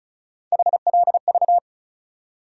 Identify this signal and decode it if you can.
Morse code, heard as a keyed tone, about 36 wpm, 700 Hz. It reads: HLV
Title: USA Call Sign Postfixes - 3 Letters 35wpm